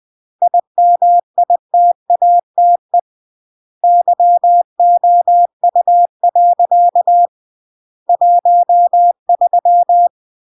Morse code, heard as a keyed tone, about 20 wpm, 700 Hz. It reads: IMITATE YOU. 13
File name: CW / jodorowsky / 82 commandments 20-20